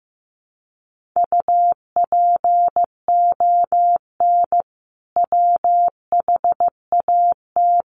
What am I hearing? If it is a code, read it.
Morse code, heard as a keyed tone, about 15 wpm, 700 Hz. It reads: UPON WHAT
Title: Sentences from Top 1500 Words 15wpm